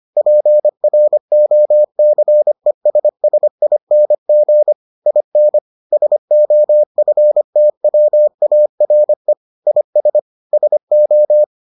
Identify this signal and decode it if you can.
Morse code, heard as a keyed tone, about 25 wpm, 600 Hz. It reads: PROCESSING IN SOFTWARE IS SO